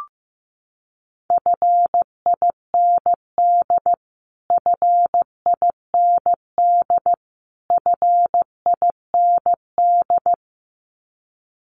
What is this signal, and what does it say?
Morse code, heard as a keyed tone, about 15 wpm, 700 Hz. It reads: FIND FIND FIND
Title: Top 200 Words Repeated 15wpm